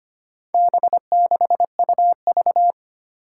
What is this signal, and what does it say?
Morse code, heard as a keyed tone, about 25 wpm, 700 Hz. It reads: B6UV
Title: Four Challenging Letter-Number Combinations 25wpm